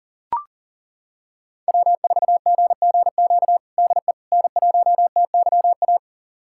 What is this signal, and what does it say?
Morse code, heard as a keyed tone, about 40 wpm, 700 Hz. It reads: W4GGQ DE N1TYA
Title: USA Callsign DE Callsign - 5 Characters - 40wpm